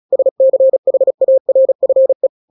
Morse code, heard as a keyed tone, about 36 wpm, 530 Hz. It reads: SCHARFE
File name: daily / 35wpm